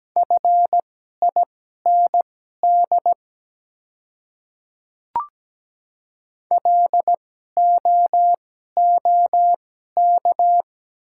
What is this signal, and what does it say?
Morse code, heard as a keyed tone, about 17 wpm, spaced as for 12 wpm, 700 Hz. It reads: FIND LOOK
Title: 2x Character Spacing - Top 200 Words - 17wpm